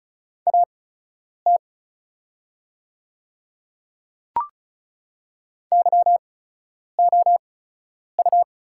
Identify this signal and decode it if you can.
Morse code, heard as a keyed tone, about 36 wpm, spaced as for 10 wpm, 700 Hz. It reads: AT YOU
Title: 8x Character Spacing - Top 100 Words - 35wpm